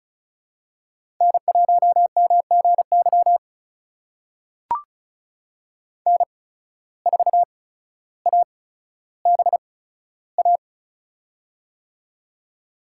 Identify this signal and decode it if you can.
Morse code, heard as a keyed tone, about 36 wpm, spaced as for 15 wpm, 700 Hz. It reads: N1MGY N4ABA
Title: USA Call Signs - 5 Characters - 8x Character Spacing - 35wpm